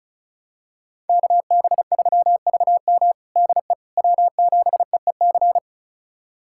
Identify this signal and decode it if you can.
Morse code, heard as a keyed tone, about 36 wpm, 700 Hz. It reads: KB3VM DE W7EEC
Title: USA Callsign DE Callsign - 5 Characters - 35wpm